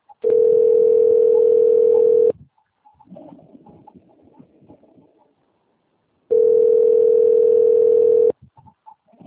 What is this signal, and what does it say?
Ringback tone (North America), 440 + 480 Hz.